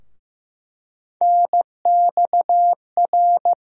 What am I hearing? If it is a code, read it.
Morse code, heard as a keyed tone, about 15 wpm, 700 Hz. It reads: NXR
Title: USA Call Sign Postfixes - 3 Letters 15wpm